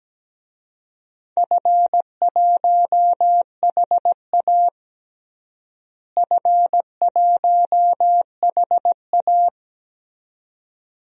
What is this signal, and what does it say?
Morse code, heard as a keyed tone, about 17 wpm, 700 Hz. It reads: F1HA F1HA
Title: DXCC-100 Call Signs - 4 Characters - Repeated 2x - 17wpm